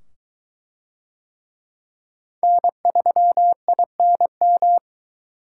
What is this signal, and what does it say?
Morse code, heard as a keyed tone, about 21 wpm, 700 Hz. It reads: N3INM